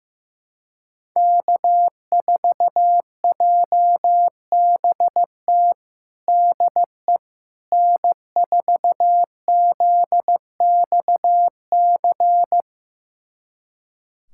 Morse code, heard as a keyed tone, about 15 wpm, 700 Hz. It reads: K4JBT DE N4ZXC
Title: USA Callsign DE Callsign - 5 Characters - 15wpm